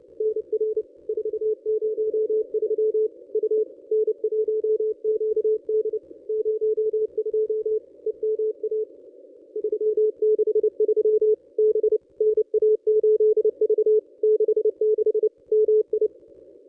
Morse code, heard as a keyed tone, about 30 wpm, 430 Hz. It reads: NR 403 U N1QD 02 EMA 363 B NA8V 66 MI
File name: contests / ss-cw-k5tr-op-n5rz -1000